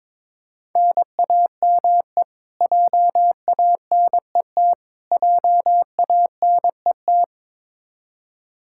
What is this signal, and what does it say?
Morse code, heard as a keyed tone, about 22 wpm, 700 Hz. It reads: NAME JANET JANET
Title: Practice QSO - Exchange 1 - Beginning - 22wpm